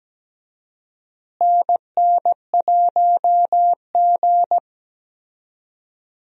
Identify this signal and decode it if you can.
Morse code, heard as a keyed tone, about 17 wpm, 700 Hz. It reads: NN1G
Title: USA Call Signs - 4 Characters - Repeated 3x - 17wpm